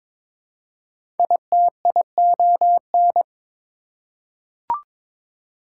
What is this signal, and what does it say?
Morse code, heard as a keyed tone, about 22 wpm, 700 Hz. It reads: ITION
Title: Prefixes and Suffixes 22wpm